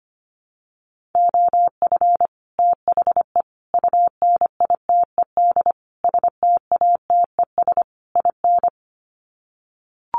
Morse code, heard as a keyed tone, about 25 wpm, 700 Hz. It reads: OF THE UNITED STATES IN
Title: Sets of 5 Words Encore 25wpm